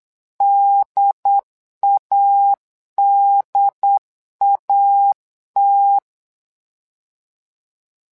Morse code, heard as a keyed tone, about 8 wpm, 800 Hz. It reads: DADAT
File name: LEZIONI / 2 UHDN / 2 UHD 8wpm 2258